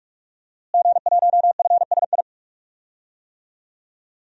Morse code, heard as a keyed tone, about 46 wpm, 700 Hz. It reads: G1FSI
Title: DXCC-100 Call Signs - 5 Characters - Repeated 2x - 45wpm